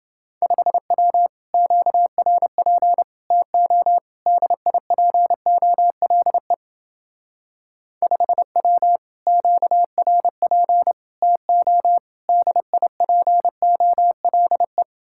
Morse code, heard as a keyed tone, about 28 wpm, 700 Hz. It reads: 5W QRP TO DIPOLE 5W QRP TO DIPOLE